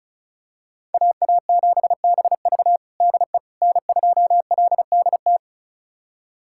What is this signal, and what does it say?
Morse code, heard as a keyed tone, about 36 wpm, 700 Hz. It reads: AA7BV DE N2LDT
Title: USA Callsign DE Callsign - 5 Characters - 35wpm